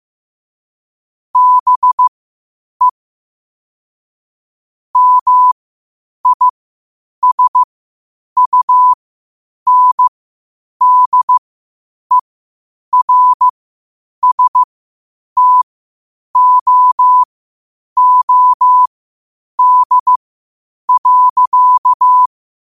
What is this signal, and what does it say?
Morse code, heard as a keyed tone, about 15 wpm, spaced as for 8 wpm, 1000 Hz. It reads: BE MISUNDERSTOOD.
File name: qotdfiles / 8wpm